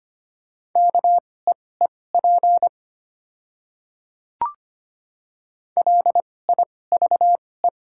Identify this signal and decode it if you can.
Morse code, heard as a keyed tone, about 25 wpm, spaced as for 18 wpm, 700 Hz. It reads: KEEP LIVE
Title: 2x Character Spacing - Top 300 Words - 25wpm